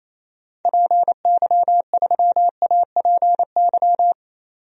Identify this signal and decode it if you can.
Morse code, heard as a keyed tone, about 28 wpm, 700 Hz. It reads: PY3APY
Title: DXCC-100 Call Signs - 6 Characters - 28wpm